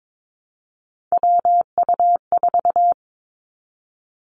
Morse code, heard as a keyed tone, about 22 wpm, 700 Hz. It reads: WU4
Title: Three Challenging Letter-Number Combinations 22wpm